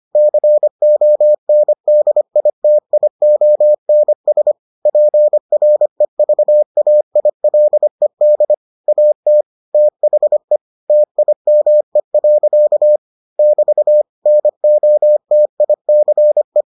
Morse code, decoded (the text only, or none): CONDITIONS PREVAILED AT THE TIME. <BT> NOTICE